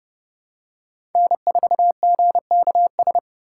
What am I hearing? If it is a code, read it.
Morse code, heard as a keyed tone, about 30 wpm, 700 Hz. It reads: N4GKS
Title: USA Call Signs - 5 Characters - 4x Character Spacing - 30wpm